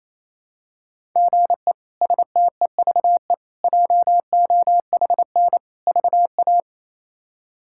Morse code, heard as a keyed tone, about 27 wpm, 700 Hz. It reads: GE STEVE JOHN VA
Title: K1USN SST QSO Practice - 28wpm